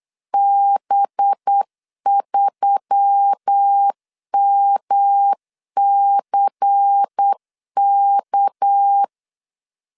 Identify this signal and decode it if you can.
Morse code, heard as a keyed tone, about 8 wpm, 800 Hz. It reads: B3MCK
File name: 8wpm